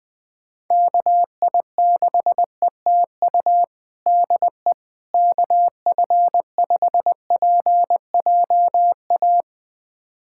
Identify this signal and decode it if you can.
Morse code, heard as a keyed tone, about 20 wpm, 700 Hz. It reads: KI6ETU DE KF5PJA
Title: USA Callsign DE Callsign - 6 Characters - 20wpm